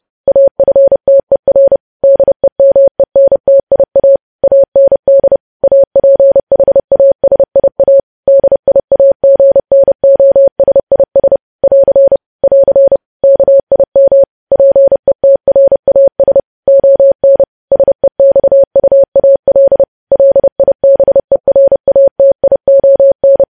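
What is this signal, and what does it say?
Morse code, heard as a keyed tone, about 30 wpm, 550 Hz. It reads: AFTER DEMENTIA AND APHASIA DIAGNOSIS <AR> <AR> KIM PETRAS ON SEXUAL LIBERATION
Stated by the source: News headlines sent at 30 wpm.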